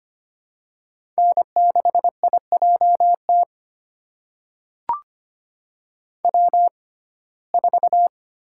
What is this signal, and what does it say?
Morse code, heard as a keyed tone, about 25 wpm, spaced as for 18 wpm, 700 Hz. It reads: N6IJT W4
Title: USA Call Signs - 5 Characters - 6x Character Spacing - 25wpm